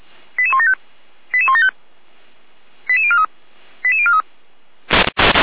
Five-tone selective call (ZVEI), heard as quick runs of five tones, tones of about 70 ms. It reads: 88175 88175 88043 88043